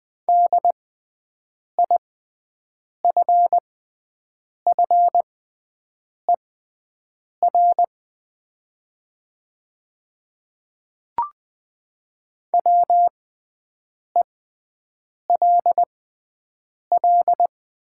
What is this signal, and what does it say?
Morse code, heard as a keyed tone, about 20 wpm, spaced as for 7 wpm, 700 Hz. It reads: DIFFER WELL
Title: 6x Character Spacing - Top 200 Words - 20wpm